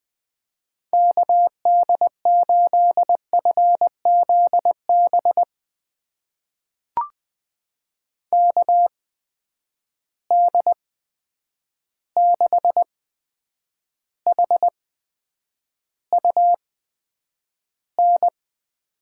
Morse code, heard as a keyed tone, about 20 wpm, spaced as for 9 wpm, 700 Hz. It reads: KD8FZB KD6HUN
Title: USA Call Signs - 6 Characters - 8x Character Spacing - 20wpm